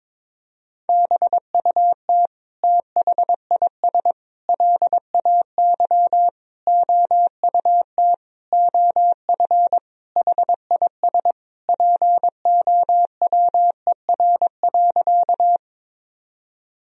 Morse code, heard as a keyed tone, about 21 wpm, 700 Hz. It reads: BUT THIS LAY OUT OF HIS POWER.
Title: Sentences from Top 1000 Words - Review Part 2 - 22wpm